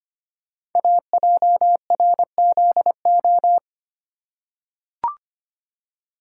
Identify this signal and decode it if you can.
Morse code, heard as a keyed tone, about 25 wpm, 700 Hz. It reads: AJRZO
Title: Code Groups - 5 Letter:Numbers - 25wpm